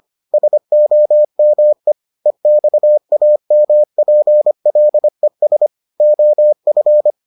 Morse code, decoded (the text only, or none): SOME EXAMPLES OF